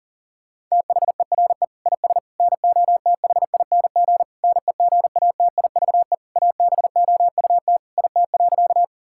Morse code, heard as a keyed tone, about 40 wpm, 700 Hz. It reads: THERE IS NOTHING NEGATIVE ABOUT IT.
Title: Sentences from Top 1000 Words - Review Part 6 - 40wpm